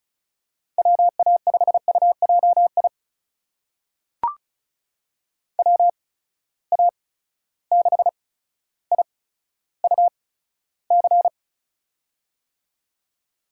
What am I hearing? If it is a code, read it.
Morse code, heard as a keyed tone, about 36 wpm, spaced as for 15 wpm, 700 Hz. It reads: WA5UJI WA6IUC
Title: USA Call Signs - 6 Characters - 8x Character Spacing - 35wpm